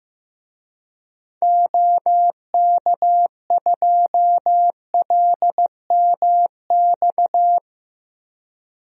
Morse code, heard as a keyed tone, about 15 wpm, 700 Hz. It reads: OK2LMX
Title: DXCC-100 Call Signs - 6 Characters - Repeated 3x - 15wpm